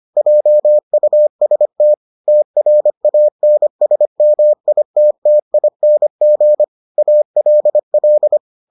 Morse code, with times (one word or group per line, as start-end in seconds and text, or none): 0.2-1.9 JUST
2.3-6.6 TRANSMITTING
7.0-8.4 ALL